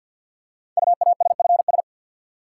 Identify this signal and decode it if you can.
Morse code, heard as a keyed tone, about 46 wpm, 700 Hz. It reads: UASFS
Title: Code Groups - 5 Letters - 50wpm